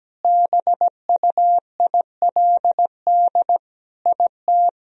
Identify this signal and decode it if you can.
Morse code, heard as a keyed tone, about 17 wpm, 700 Hz. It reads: BUILD IT
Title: Sentences from Top 1000 Words - Review Part 1 - 17wpm